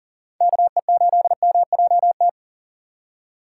Morse code, heard as a keyed tone, about 38 wpm, 700 Hz. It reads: KE8MJT
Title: USA Call Signs - 6 Characters - 6x Character Spacing - 40wpm